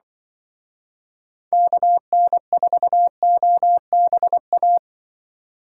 Morse code, heard as a keyed tone, about 24 wpm, 700 Hz. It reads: KN4OBA
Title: Speed Racing - USA 2x3 Call Signs 20wpm